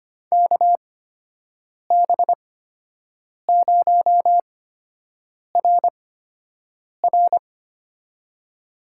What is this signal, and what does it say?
Morse code, heard as a keyed tone, about 25 wpm, spaced as for 7 wpm, 700 Hz. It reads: KB0RR
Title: USA Call Signs - 5 Characters - 8x Character Spacing - 25wpm